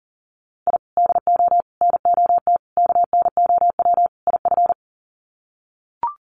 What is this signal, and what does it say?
Morse code, heard as a keyed tone, about 40 wpm, 700 Hz. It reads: I DO NOT KNOW IF
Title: Sets of 5 Words 40wpm